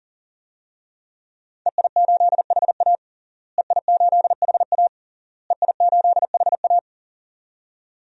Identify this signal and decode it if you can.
Morse code, heard as a keyed tone, about 40 wpm, 700 Hz. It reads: EI8HA EI8HA EI8HA
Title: DXCC-100 Call Signs - 5 Characters - Repeated 3x - 40wpm